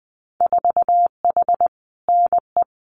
Morse code, heard as a keyed tone, about 20 wpm, 700 Hz. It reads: 4H NE
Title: Winter Field Day Exchanges Encore 20wpm